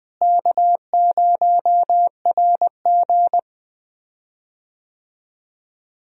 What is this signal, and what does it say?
Morse code, heard as a keyed tone, about 20 wpm, 700 Hz. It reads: K0RG
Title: USA Call Signs - 4 Characters - Repeated 2x - 20wpm